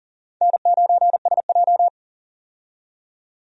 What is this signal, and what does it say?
Morse code, heard as a keyed tone, about 37 wpm, 700 Hz. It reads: N9SJ
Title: USA Call Signs - 4 Characters - 40wpm